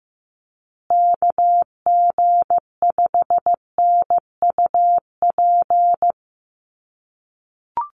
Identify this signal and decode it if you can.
Morse code, heard as a keyed tone, about 15 wpm, 700 Hz. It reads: KG5NUP
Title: USA Call Signs Repeated 2x 15wpm